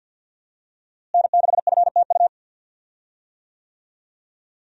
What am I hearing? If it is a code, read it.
Morse code, heard as a keyed tone, about 50 wpm, 700 Hz. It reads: N6FTU
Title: USA Call Signs - 5 Characters - Repeated 3x - 50wpm